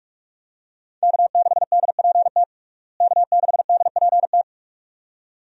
Morse code, heard as a keyed tone, about 42 wpm, 700 Hz. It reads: K6DPT K6DPT
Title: USA Call Signs - 5 Characters - Repeated 2x - 45wpm